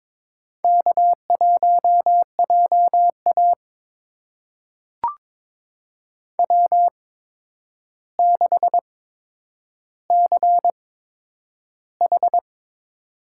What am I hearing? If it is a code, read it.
Morse code, heard as a keyed tone, about 22 wpm, spaced as for 9 wpm, 700 Hz. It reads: K1JA W6CH
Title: USA Call Signs - 4 Characters - 8x Character Spacing - 22wpm